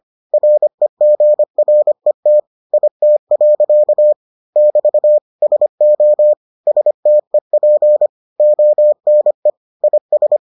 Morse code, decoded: REGRET IT. <BT> SO STEP ONE IS

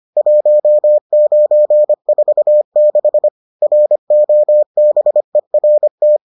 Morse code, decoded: 1946 ROBERT

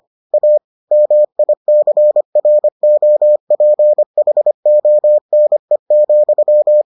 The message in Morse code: A MICROPHONE,

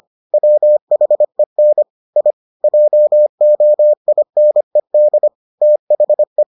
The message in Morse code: WHEN I JOINED THE

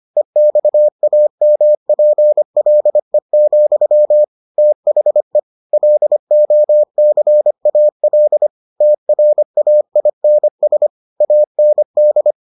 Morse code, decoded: EXAMPLE, THE LOCAL TRAINS AND

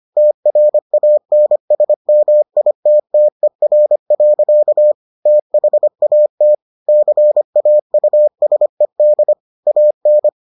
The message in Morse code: TRANSMITTER. THAT CAUSED AN

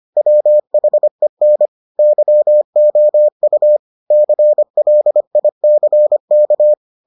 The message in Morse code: WHEN YOU CLICK

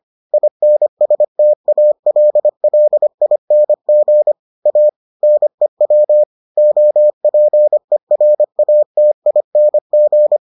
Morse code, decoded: INSTALLING A NEW OPERATING

